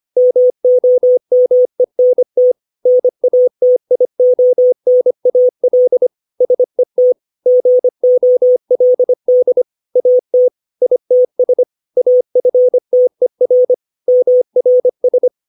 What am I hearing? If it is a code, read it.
Morse code, heard as a keyed tone, about 25 wpm, 500 Hz. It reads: MOMENT NATIONAL SET GOLD AT ITS AFTER MRS